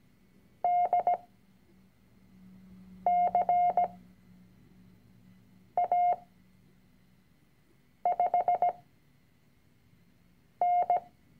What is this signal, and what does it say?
Morse code, heard as a keyed tone, about 17 wpm, spaced as for 4 wpm, 710 Hz. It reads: DCA5N
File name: CW-Online-Kurs / Anfänger / Lektion 006